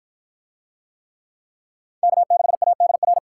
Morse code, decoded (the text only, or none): K6ADR